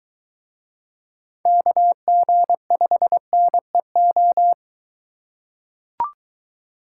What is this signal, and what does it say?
Morse code, heard as a keyed tone, about 23 wpm, 700 Hz. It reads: KG5NEO